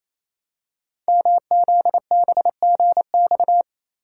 Morse code, heard as a keyed tone, about 27 wpm, 700 Hz. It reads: MZBGX
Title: Code Groups - 5 Letters - 28wpm